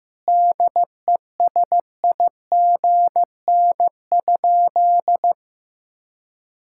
Morse code, heard as a keyed tone, about 15 wpm, 700 Hz. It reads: DESIGN?
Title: Sentences from Top 1000 Words - Review Part 5 - 15wpm